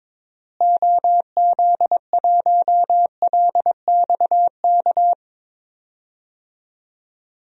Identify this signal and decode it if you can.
Morse code, heard as a keyed tone, about 22 wpm, 700 Hz. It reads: OZ1LXK
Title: DXCC-100 Call Signs - 6 Characters - Repeated 2x - 22wpm